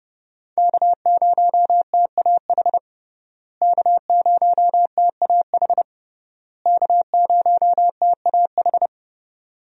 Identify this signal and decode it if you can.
Morse code, heard as a keyed tone, about 30 wpm, 700 Hz. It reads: K0TAH K0TAH K0TAH
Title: USA Call Signs - 5 Characters - Repeated 3x - 30wpm